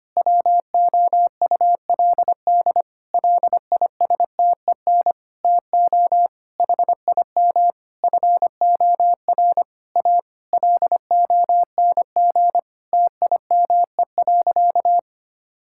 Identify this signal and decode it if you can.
Morse code, heard as a keyed tone, about 25 wpm, 700 Hz. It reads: WOULD LISTEN TO HIM FOR A LONG TIME.
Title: Sentences from Top 1600 Words 25wpm